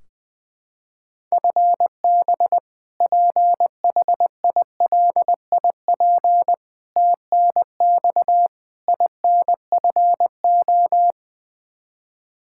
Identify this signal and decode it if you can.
Morse code, heard as a keyed tone, about 21 wpm, 700 Hz. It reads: FB PHILIP TNX INFO
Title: Practice QSO - Exchange 3 - More Info - 20wpm